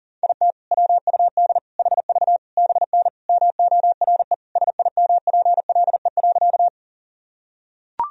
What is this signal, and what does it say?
Morse code, heard as a keyed tone, about 40 wpm, 700 Hz. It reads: IT WUD HV BN MORE SIMPLE.
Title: Sentences from Top 400 Words and Abbreviations 40wpm